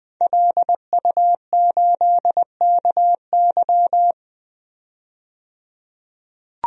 Morse code, heard as a keyed tone, about 20 wpm, 700 Hz. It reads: LU8KY
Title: DXCC-100 Call Signs - 5 Characters - 20wpm